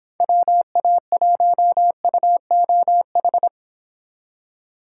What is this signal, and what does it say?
Morse code, heard as a keyed tone, about 26 wpm, 700 Hz. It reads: WA1UOH